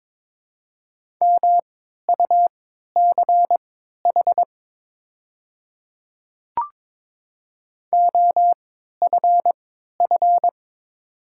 Morse code, heard as a keyed tone, about 22 wpm, spaced as for 13 wpm, 690 Hz. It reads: MUCH OFF
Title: 3x Character Spacing - Top 300 Words - 22wpm